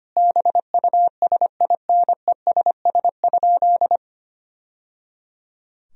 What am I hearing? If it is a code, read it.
Morse code, heard as a keyed tone, about 25 wpm, 700 Hz. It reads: BUSINESS?
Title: Sentences from Top 1000 Words - Review Part 4 - 25wpm